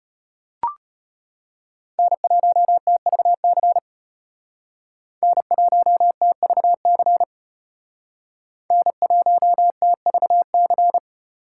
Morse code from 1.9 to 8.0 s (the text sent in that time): N1TVC N1TVC